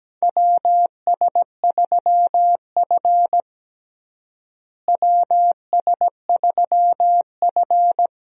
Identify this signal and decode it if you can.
Morse code, heard as a keyed tone, about 17 wpm, 700 Hz. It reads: WS3F WS3F
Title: USA Call Signs - 4 Characters - Repeated 2x - 17wpm